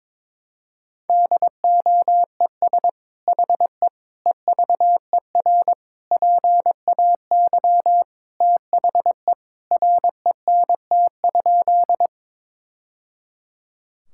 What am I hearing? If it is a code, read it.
Morse code, heard as a keyed tone, about 22 wpm, 700 Hz. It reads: DOES HE EVER PAY THE RENT?
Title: Sentences from Top 1000 Words - Review Part 5 - 22wpm